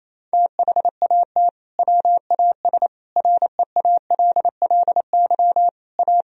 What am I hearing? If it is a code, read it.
Morse code, heard as a keyed tone, about 28 wpm, 700 Hz. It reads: THAT WAS REALLY A
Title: Sentences from Top 1000 Words - Review Part 1 - 28wpm